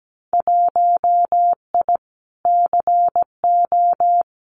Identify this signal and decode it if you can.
Morse code, heard as a keyed tone, about 17 wpm, 700 Hz. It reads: 1I CO
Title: Winter Field Day Exchanges Encore 17wpm